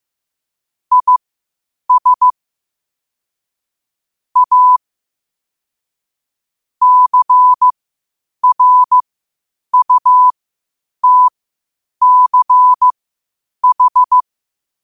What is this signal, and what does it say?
Morse code, heard as a keyed tone, about 15 wpm, spaced as for 8 wpm, 1000 Hz. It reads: IS A CRUTCH
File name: qotdfiles / 8wpm